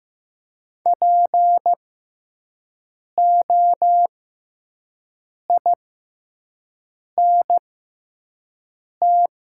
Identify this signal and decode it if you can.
Morse code, heard as a keyed tone, about 15 wpm, spaced as for 5 wpm, 700 Hz. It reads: POINT
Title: 6x Character Spacing - Top 300 Words - 15wpm